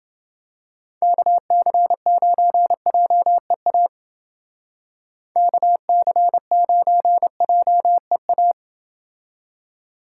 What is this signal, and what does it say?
Morse code, heard as a keyed tone, about 28 wpm, 700 Hz. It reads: KC9JEA KC9JEA